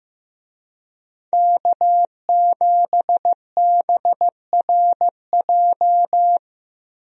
Morse code, heard as a keyed tone, about 15 wpm, 700 Hz. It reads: K7BRJ
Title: USA Call Signs - 5 Characters - 15wpm